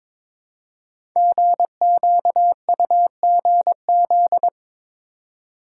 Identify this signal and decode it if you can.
Morse code, heard as a keyed tone, about 22 wpm, 700 Hz. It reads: GQUGZ
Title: Code Groups - 5 Letters - 22wpm